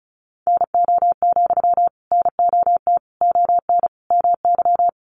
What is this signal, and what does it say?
Morse code, heard as a keyed tone, about 34 wpm, 700 Hz. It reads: NO, NOT ON MY
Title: Sentences from Top 900 Words 35wpm